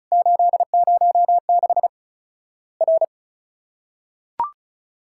Morse code, heard as a keyed tone, about 36 wpm, 700 Hz. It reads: 806 R
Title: Worked All Europe DX Contest - QTC Exchange - 35wpm